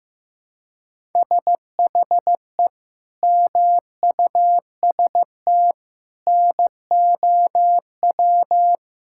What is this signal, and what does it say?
Morse code, heard as a keyed tone, about 16 wpm, 700 Hz. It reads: SHE MUST NOW
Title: Sentences from Top 1600 Words 15wpm